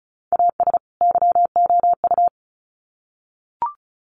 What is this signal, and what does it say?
Morse code, heard as a keyed tone, about 34 wpm, 700 Hz. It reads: AS YOU
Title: Sets of 2 Words 35wpm